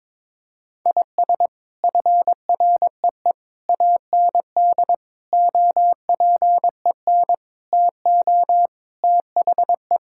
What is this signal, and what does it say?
Morse code, heard as a keyed tone, about 22 wpm, 700 Hz. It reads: IS FREE AND OPEN TO THE
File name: practice / 22wpm / Sets of 7 Words - 22wpm